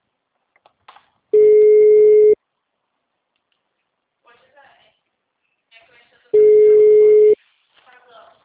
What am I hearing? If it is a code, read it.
Ringback tone (Europe, 425 Hz), with speech.